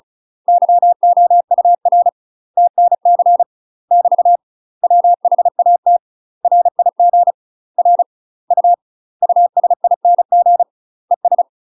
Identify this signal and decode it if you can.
Morse code, heard as a keyed tone, about 36 wpm, 700 Hz. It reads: YOUR TNC <BT> WHAT RIG R U USING ES
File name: UK MASTER QSOs TRAINING / 35WPM / UK MASTER QSOs 0033